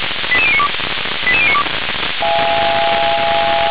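Five-tone selective call (ZVEI), heard as quick runs of five tones, tones of about 70 ms. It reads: 88092 88092
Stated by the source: ZVEI five-tone selective call received via SDR.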